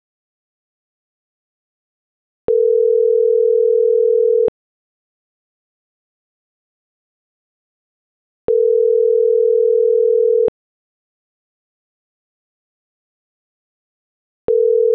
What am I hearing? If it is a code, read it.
Ringback tone (North America), 440 + 480 Hz.